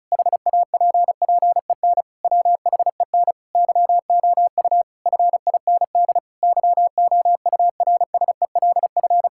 Morse code, heard as a keyed tone, about 36 wpm, 700 Hz. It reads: HAPPEN WHEN YOU FIND YOURSELF
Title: Sentences from Top 1000 Words - Review Part 5 - 35wpm